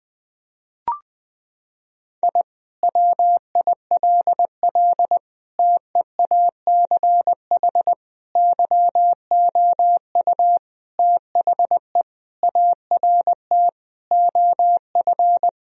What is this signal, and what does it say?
Morse code, heard as a keyed tone, about 20 wpm, 700 Hz. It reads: I WILL TEACH YOU THE ART OF
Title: Sentences from Top 1000 Words - Review Part 3 - 20wpm